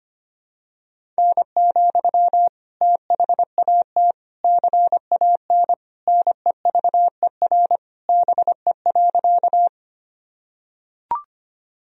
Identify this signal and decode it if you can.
Morse code, heard as a keyed tone, about 25 wpm, 700 Hz. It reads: N, THAT CAN NEVER BE.
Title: Sentences from Top 300 Words and Abbreviations 25wpm